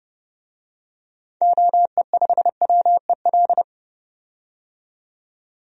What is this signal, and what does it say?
Morse code, heard as a keyed tone, about 31 wpm, 700 Hz. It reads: OE5WEL
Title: DXCC-100 Call Signs - 6 Characters - 30wpm